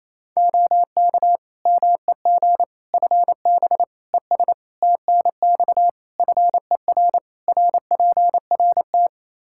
Morse code, heard as a keyed tone, about 28 wpm, 700 Hz. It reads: OK MEG FB ES TNX FER RPRT
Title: Practice QSO - Exchange 2 - Info - 28wpm